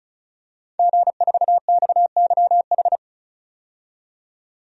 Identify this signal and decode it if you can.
Morse code, heard as a keyed tone, about 36 wpm, 700 Hz. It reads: G4XYH
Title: DXCC-100 Call Signs - 5 Characters - 35wpm